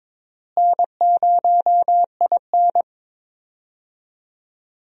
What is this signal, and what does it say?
Morse code, heard as a keyed tone, about 22 wpm, 700 Hz. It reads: N0IN
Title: USA Call Signs - 4 Characters - Repeated 2x - 22wpm